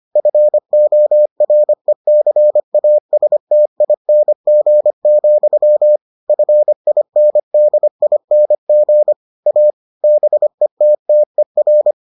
Morse code, heard as a keyed tone, about 25 wpm, 600 Hz. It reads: FORECASTING, FINDING A BETTER